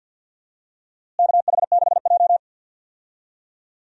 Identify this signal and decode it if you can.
Morse code, heard as a keyed tone, about 51 wpm, 700 Hz. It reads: KH6J
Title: DXCC-100 Call Signs - 4 Characters - Repeated 2x - 50wpm